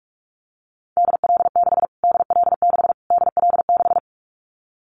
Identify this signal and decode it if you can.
Morse code, heard as a keyed tone, about 44 wpm, 700 Hz. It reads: DL6 DL6 DL6
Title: Callsign Prefixes for Top 25 DXCC - Repeated 3x - 45wpm